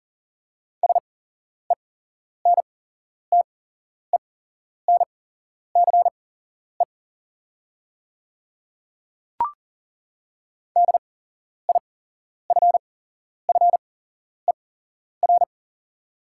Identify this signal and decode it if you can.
Morse code, heard as a keyed tone, about 38 wpm, spaced as for 11 wpm, 690 Hz. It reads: SENTENCE DIFFER
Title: 8x Character Spacing - Top 300 Words - 40wpm